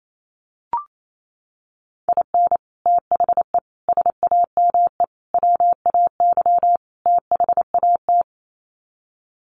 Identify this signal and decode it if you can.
Morse code, heard as a keyed tone, about 28 wpm, 700 Hz. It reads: IN THE SAME WAY THAT
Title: Sets of 5 Words 28wpm